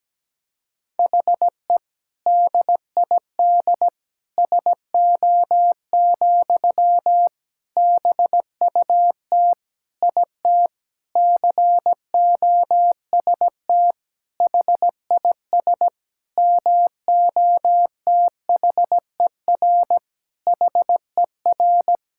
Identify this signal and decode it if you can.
Morse code, heard as a keyed tone, about 17 wpm, 700 Hz. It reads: HE DID SO, BUT IT COST HIS MOTHER HER
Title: Sentences from Top 1000 Words - Review Part 5 - 17wpm